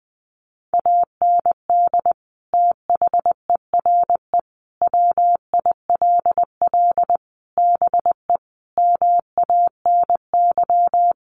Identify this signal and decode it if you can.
Morse code, heard as a keyed tone, about 20 wpm, 700 Hz. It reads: AND THERE WILL BE MANY
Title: Sentences from Top 1500 Words 20wpm